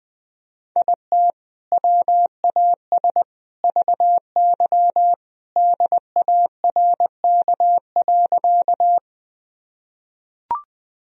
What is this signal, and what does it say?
Morse code, heard as a keyed tone, about 20 wpm, 700 Hz. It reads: IT WAS VY DARK.
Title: Sentences from Top 500 Words and Abbreviations 20wpm